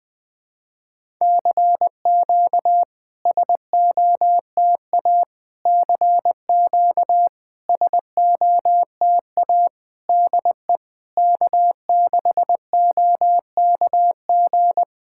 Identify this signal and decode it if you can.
Morse code, heard as a keyed tone, about 20 wpm, 690 Hz. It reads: CQ SOTA CQ SOTA DE K6OKG
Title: SOTA QSO Practice 20wpm